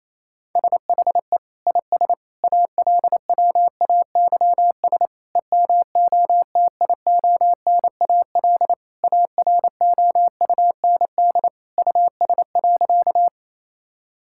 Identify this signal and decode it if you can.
Morse code, heard as a keyed tone, about 28 wpm, 700 Hz. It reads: SHE IS ALWAYS EMOTIONAL AROUND US.
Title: Sentences from Top 1000 Words - Review Part 5 - 28wpm